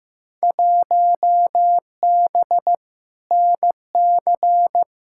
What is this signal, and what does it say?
Morse code, heard as a keyed tone, about 15 wpm, 700 Hz. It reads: 1B NC
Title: Field Day Exchanges - 15wpm